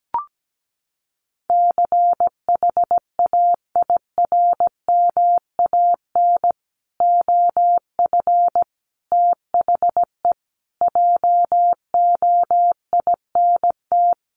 Morse code, heard as a keyed tone, about 17 wpm, 700 Hz. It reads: CHAIRMAN OF THE JOINT
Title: Sets of 5 Words Encore 17wpm